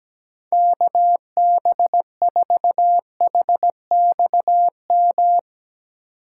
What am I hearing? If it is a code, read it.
Morse code, heard as a keyed tone, about 17 wpm, 700 Hz. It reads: KB4HXM
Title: Speed Racing - USA 2x3 Call Signs 17wpm